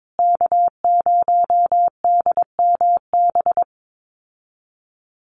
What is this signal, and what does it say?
Morse code, heard as a keyed tone, about 22 wpm, 700 Hz. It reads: K0DMB
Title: USA Call Signs Repeated 2x 22wpm